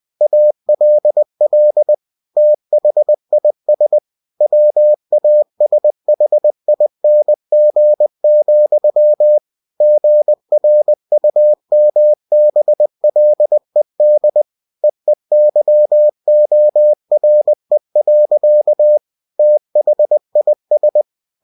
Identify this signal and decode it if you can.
Morse code, heard as a keyed tone, about 20 wpm, 600 Hz. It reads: ALL THIS WASHING, GRUMBLED EEYORE. THIS
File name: cw-books / winnie-the-pooh / winnie-20wpm-page-0085